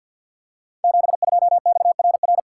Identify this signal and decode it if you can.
Morse code, heard as a keyed tone, about 50 wpm, 700 Hz. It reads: 72XRR
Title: Code Groups - 5 Letter:Numbers - 50wpm